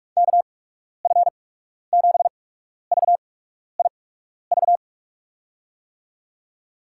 Morse code, heard as a keyed tone, about 44 wpm, spaced as for 12 wpm, 700 Hz. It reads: KF7VIV
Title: USA Call Signs - 6 Characters - 8x Character Spacing - 45wpm